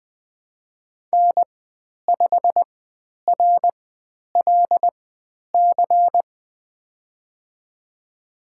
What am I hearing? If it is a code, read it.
Morse code, heard as a keyed tone, about 20 wpm, spaced as for 10 wpm, 700 Hz. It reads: N5RLC
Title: 102 Lesson 17g - Letter u - Call Signs 20wpm